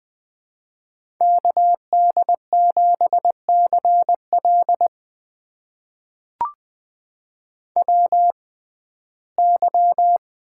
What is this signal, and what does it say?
Morse code, heard as a keyed tone, about 20 wpm, spaced as for 14 wpm, 700 Hz. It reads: KD7CL WY
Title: USA Call Signs - 6x Character Spacing - 20wpm